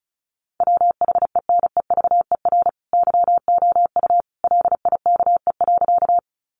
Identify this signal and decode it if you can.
Morse code, heard as a keyed tone, about 34 wpm, 700 Hz. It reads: WHENEVER YOU LIKE.
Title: Sentences from Top 1200 Words 35wpm